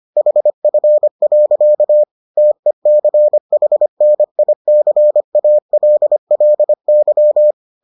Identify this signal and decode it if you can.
Morse code, heard as a keyed tone, about 25 wpm, 600 Hz. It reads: HF. TECHNICALLY